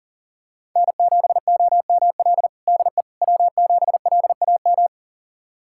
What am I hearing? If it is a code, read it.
Morse code, heard as a keyed tone, about 40 wpm, 700 Hz. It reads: N7OML DE W7LAM
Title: USA Callsign DE Callsign - 5 Characters - 40wpm